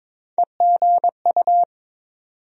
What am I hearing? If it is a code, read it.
Morse code, heard as a keyed tone, about 21 wpm, 700 Hz. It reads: EGU
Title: USA Call Sign Postfixes - 3 Letters 22wpm